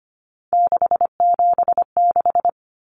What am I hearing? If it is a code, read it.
Morse code, heard as a keyed tone, about 25 wpm, 700 Hz. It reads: 676
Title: Numbers - 3-Digit 25wpm